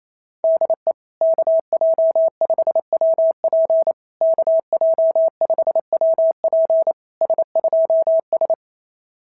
Morse code, heard as a keyed tone, about 27 wpm, 650 Hz. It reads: DE KJ5WP KJ5WP S2S
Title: SOTA QSO Practice 28wpm